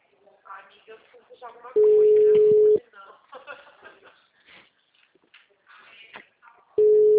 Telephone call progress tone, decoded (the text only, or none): ringback tone